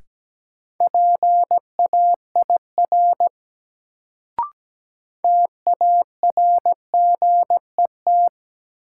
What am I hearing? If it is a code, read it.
Morse code, heard as a keyed tone, about 17 wpm, 700 Hz. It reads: PAIR TARGET
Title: Sentences from Top 1200 Words - New Word Drill - 17wpm